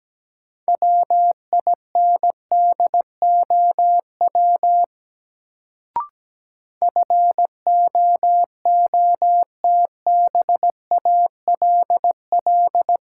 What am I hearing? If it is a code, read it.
Morse code, heard as a keyed tone, about 17 wpm, 700 Hz. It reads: WINDOW FOOTBALL
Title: Sentences from Top 1600 Words - New Word Drill Alt - 17wpm